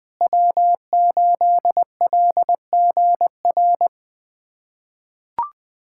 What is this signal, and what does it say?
Morse code, heard as a keyed tone, about 20 wpm, 700 Hz. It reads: W8LGR
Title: USA Callsign DE Callsign - 5 Characters - 20wpm